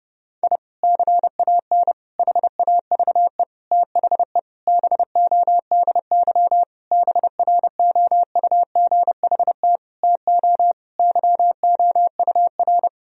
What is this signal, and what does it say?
Morse code, heard as a keyed tone, about 30 wpm, 700 Hz. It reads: I CAN HAVE THE BODY BROUGHT TO YOUR
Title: Sentences from Top 1000 Words - Review Part 2 - 30wpm